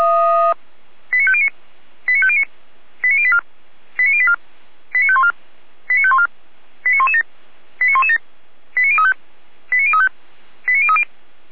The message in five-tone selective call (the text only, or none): 89509 89509 89084 89084 89514 89514 89117 89117 89035 89035 89030